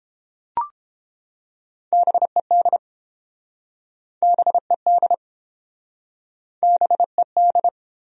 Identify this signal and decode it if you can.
Morse code, heard as a keyed tone, about 29 wpm, 700 Hz. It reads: BED BED BED